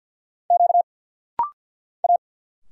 Morse code, heard as a keyed tone, about 50 wpm, 690 Hz. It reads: Q A
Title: Single Letters - Rapid-Fire 50wpm